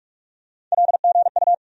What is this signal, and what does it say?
Morse code, heard as a keyed tone, about 46 wpm, 700 Hz. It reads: LGU
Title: USA Call Sign Postfixes - 3 Letters 45wpm